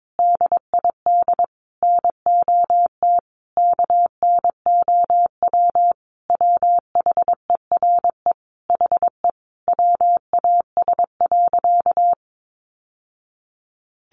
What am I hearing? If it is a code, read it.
Morse code, heard as a keyed tone, about 22 wpm, 700 Hz. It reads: DID NOT KNOW WHERE HE WAS.
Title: Sentences from Top 600 Words 22wpm